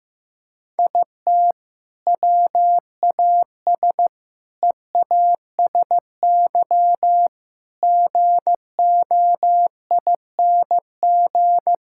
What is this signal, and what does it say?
Morse code, heard as a keyed tone, about 15 wpm, 700 Hz. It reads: IT WAS EASY GOING
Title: Sentences from Top 1000 Words - Review Part 5 - 15wpm